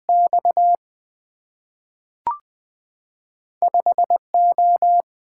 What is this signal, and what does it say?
Morse code, heard as a keyed tone, about 20 wpm, 700 Hz. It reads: X 5O
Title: Two Letter-Number Combinations - 20wpm